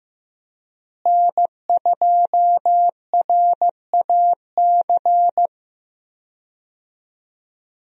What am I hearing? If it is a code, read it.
Morse code, heard as a keyed tone, about 15 wpm, 700 Hz. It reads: N2RAC
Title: USA Call Signs - 5 Characters - Repeated 3x - 15wpm